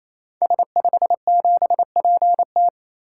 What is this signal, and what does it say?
Morse code, heard as a keyed tone, about 28 wpm, 700 Hz. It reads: S57PT
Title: Code Groups - 5 Letter:Numbers - 28wpm